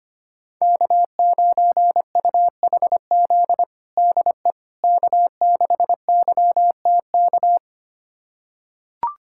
Morse code, heard as a keyed tone, about 25 wpm, 700 Hz. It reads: K9UHZ DE K6YTK
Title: USA Callsign DE Callsign - 5 Characters - 25wpm